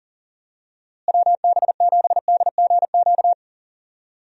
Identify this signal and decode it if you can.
Morse code, heard as a keyed tone, about 40 wpm, 700 Hz. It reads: WB7DGQ
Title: USA Call Signs - 6 Characters - Repeated 2x - 40wpm